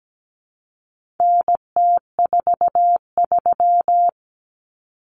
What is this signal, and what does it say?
Morse code, heard as a keyed tone, about 17 wpm, 700 Hz. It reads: NT43
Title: Common Cut Numbers - 4-Digit 17wpm